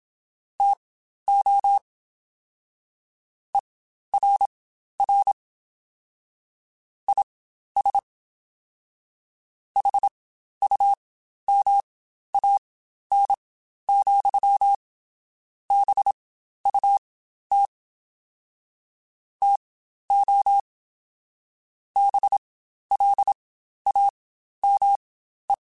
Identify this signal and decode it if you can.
Morse code, heard as a keyed tone, about 27 wpm, spaced as for 12 wpm, 800 Hz. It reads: TO ERR IS HUMAN, BUT TO BLAME
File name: Spaced / Chuckles Spaced-mp3s / HumanErr sp @25